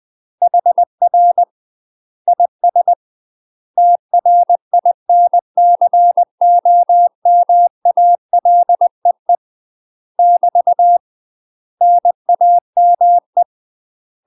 Morse code, heard as a keyed tone, about 20 wpm, 700 Hz. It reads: HR IS TRINCOMALEE <BT> NAME
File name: UK MASTER QSOs TRAINING / 20WPM / UK MASTER QSOs 0013